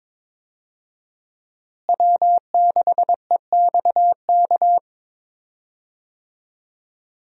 Morse code, heard as a keyed tone, about 21 wpm, 700 Hz. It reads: W6EXK
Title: USA Call Signs - 5 Characters - Repeated 2x - 22wpm